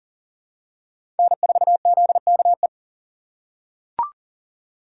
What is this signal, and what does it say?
Morse code, heard as a keyed tone, about 40 wpm, 700 Hz. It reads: N4ZKE
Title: USA Call Signs - 5 Characters - 3x Character Spacing - 40wpm